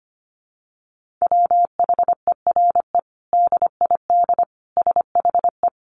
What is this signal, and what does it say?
Morse code, heard as a keyed tone, about 25 wpm, 700 Hz. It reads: WHERE DID SHE
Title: Sentences from Top 200 Words 25wpm